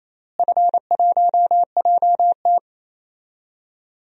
Morse code, heard as a keyed tone, about 28 wpm, 700 Hz. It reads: F1JT
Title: DXCC-100 Call Signs - 4 Characters - Repeated 3x - 28wpm